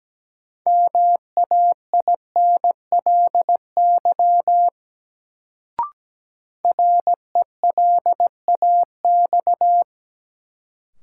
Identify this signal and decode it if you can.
Morse code, heard as a keyed tone, about 17 wpm, 700 Hz. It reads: MAINLY RELAX
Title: Sentences from Top 1400 Words - New Word Drill Alt - 17wpm